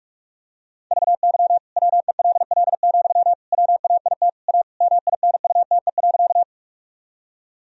Morse code, heard as a keyed tone, about 46 wpm, 690 Hz. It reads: VY WELL, WAIT A MINUTE.
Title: Sentences from Top 500 Words and Abbreviations 45wpm